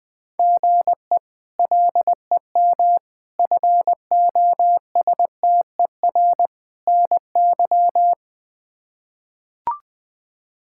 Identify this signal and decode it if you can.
Morse code, heard as a keyed tone, about 20 wpm, 700 Hz. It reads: GE LEM FOSTER NY
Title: K1USN SST QSO Practice - 20wpm